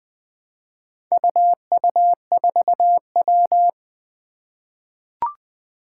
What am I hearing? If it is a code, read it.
Morse code, heard as a keyed tone, about 20 wpm, 700 Hz. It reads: UU4W
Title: Four Challenging Letter-Number Combinations 20wpm